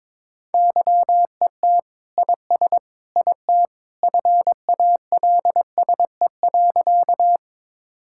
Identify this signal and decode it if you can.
Morse code, heard as a keyed tone, about 22 wpm, 700 Hz. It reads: YET IS IT FALSE.
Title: Sentences from Top 1600 Words 22wpm